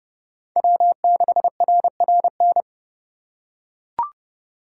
Morse code, heard as a keyed tone, about 30 wpm, 700 Hz. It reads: W6RRN